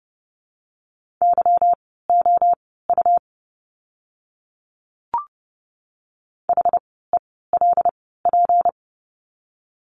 Morse code, heard as a keyed tone, about 30 wpm, spaced as for 17 wpm, 700 Hz. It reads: YOU HELP
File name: course / 30:10wpm / 237 Lesson 38f - Letter q - Words 30wpm